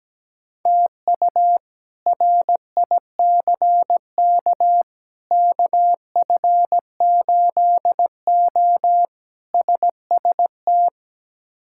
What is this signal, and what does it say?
Morse code, heard as a keyed tone, about 17 wpm, 700 Hz. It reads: TU RICK KF8O SST
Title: K1USN SST QSO Practice - 17wpm